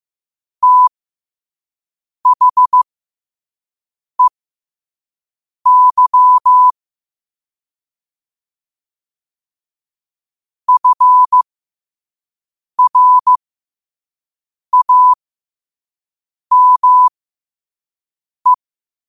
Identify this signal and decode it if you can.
Morse code, heard as a keyed tone, about 15 wpm, spaced as for 5 wpm, 1000 Hz. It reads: THEY FRAME
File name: qotdfiles / 5wpm